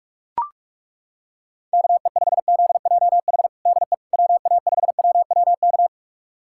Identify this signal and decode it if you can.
Morse code, heard as a keyed tone, about 42 wpm, 700 Hz. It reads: KE5ZJH DE WA5WWK
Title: USA Callsign DE Callsign - 6 Characters - 45wpm